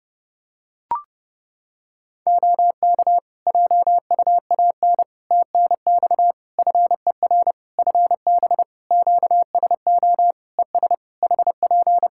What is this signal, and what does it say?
Morse code, heard as a keyed tone, about 30 wpm, 700 Hz. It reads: OK JUAN TNX FER FB QSO ES HP
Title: Practice QSO - Exchange 4 - Ending - 30wpm